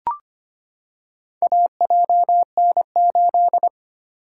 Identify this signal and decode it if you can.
Morse code, heard as a keyed tone, about 25 wpm, 700 Hz. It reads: AJN8
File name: practice / 25wpm / Four Letter-Number Combinations - 25wpm